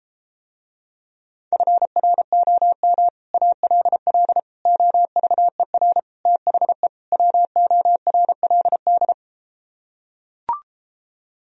Morse code, heard as a keyed tone, about 33 wpm, 700 Hz. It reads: FROM ALL OVER THE WORLD